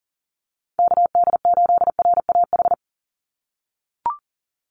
Morse code, heard as a keyed tone, about 40 wpm, 690 Hz. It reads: KD8RAH
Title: USA Call Signs 40wpm